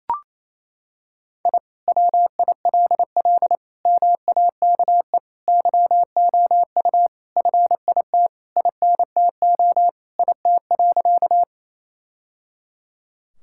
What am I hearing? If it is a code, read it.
Morse code, heard as a keyed tone, about 28 wpm, 700 Hz. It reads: I WILL MAKE YOU FIT INTO IT.
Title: Sentences from Top 1000 Words - Review Part 5 - 28wpm